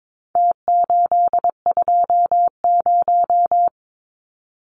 Morse code, heard as a keyed tone, about 22 wpm, 700 Hz. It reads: T820
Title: Numbers - Contest Style - 4-Digit 22wpm